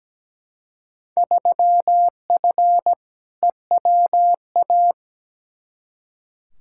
Morse code, heard as a keyed tone, about 17 wpm, 700 Hz. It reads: 3F EWA
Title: Field Day Exchanges Encore - 17wpm